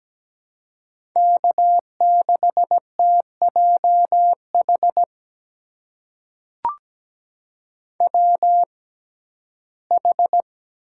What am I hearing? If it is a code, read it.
Morse code, heard as a keyed tone, about 17 wpm, spaced as for 12 wpm, 700 Hz. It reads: K6TJH WH
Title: USA Call Signs - 6x Character Spacing - 17wpm